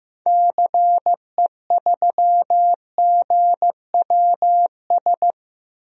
Morse code, heard as a keyed tone, about 16 wpm, 700 Hz. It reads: CE3GWS
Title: DXCC-100 Call Signs - 6 Characters - 15wpm